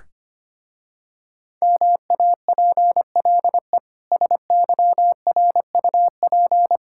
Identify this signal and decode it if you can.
Morse code, heard as a keyed tone, about 25 wpm, 700 Hz. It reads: MAPLE SYRUP
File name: practice / 25wpm / Difficult Words with Hints - 25wpm